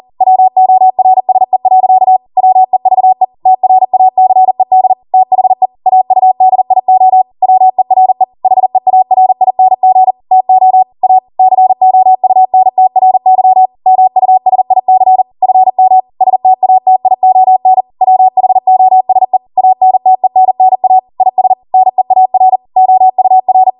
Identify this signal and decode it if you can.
Morse code, heard as a keyed tone, about 40 wpm, 750 Hz. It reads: WORSE. WEVE TRACED THE AUDIO WERE HEARING TO A COUNTRY MUSIC FM STATION WHOSE ANTENNA IS NEAR OUR
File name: Morse / 40WPM